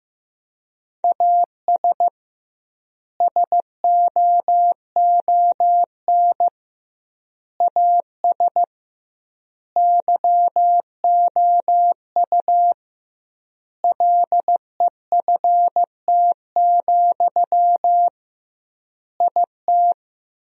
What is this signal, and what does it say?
Morse code, heard as a keyed tone, about 15 wpm, 700 Hz. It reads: AS SOON AS YOU LEFT, IT
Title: 2x Word-Spacing - Sentences from Top 400 Words - 15wpm